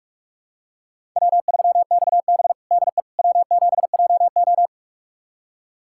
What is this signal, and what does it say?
Morse code, heard as a keyed tone, about 46 wpm, 700 Hz. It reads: W3XB DE W7JO
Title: USA Callsign DE Callsign - 4 Characters - 45wpm